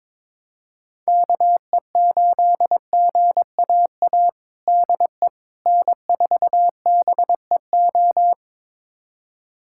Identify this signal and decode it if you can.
Morse code, heard as a keyed tone, about 22 wpm, 700 Hz. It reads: KE8GAA DE N4BEO
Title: USA Callsign DE Callsign 22wpm